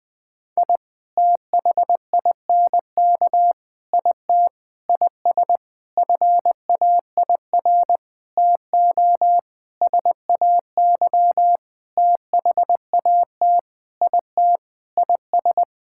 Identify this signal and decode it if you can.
Morse code, heard as a keyed tone, about 20 wpm, 700 Hz. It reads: I THINK IT IS FAIR TO SAY THAT IT IS
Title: Sentences with 8-Word Phrases - Spoken Beginning - 20wpm